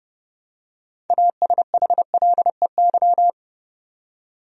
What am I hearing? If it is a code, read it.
Morse code, heard as a keyed tone, about 30 wpm, 700 Hz. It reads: ASHLEY
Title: USA Common Names - 30wpm